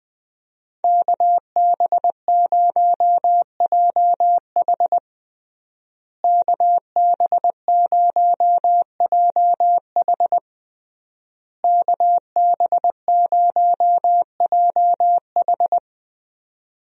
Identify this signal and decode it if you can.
Morse code, heard as a keyed tone, about 20 wpm, 700 Hz. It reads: KB0JH KB0JH KB0JH
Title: USA Call Signs - 5 Characters - Repeated 3x - 20wpm